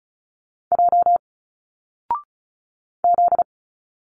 Morse code, heard as a keyed tone, about 36 wpm, 700 Hz. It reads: J Z
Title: Single Letters 35wpm